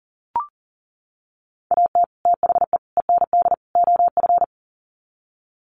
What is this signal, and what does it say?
Morse code, heard as a keyed tone, about 40 wpm, 700 Hz. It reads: AT THE END OF
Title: Sets of 4 Words 40wpm